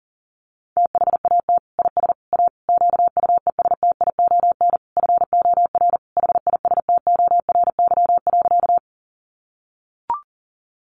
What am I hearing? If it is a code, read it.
Morse code, heard as a keyed tone, about 37 wpm, 700 Hz. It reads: THAT IS A QUESTION FOR HISTORY.
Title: Sentences from Top 600 Words 40wpm